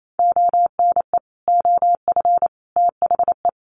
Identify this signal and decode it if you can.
Morse code, heard as a keyed tone, about 28 wpm, 700 Hz. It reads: ONE OF THE
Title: Sets of 4 Words 28wpm